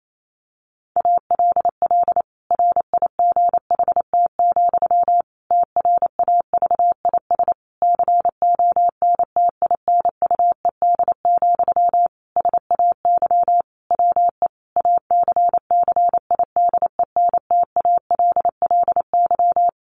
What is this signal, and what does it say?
Morse code, heard as a keyed tone, about 28 wpm, 700 Hz. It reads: ALL RIGHT, TRAVIS CONTINUED, SAY WE ACCIDENTALLY
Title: A Sound Of Thunder 28wpm